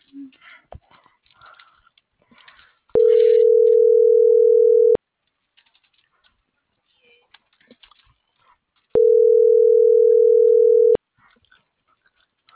Ringback tone (North America), 440 + 480 Hz.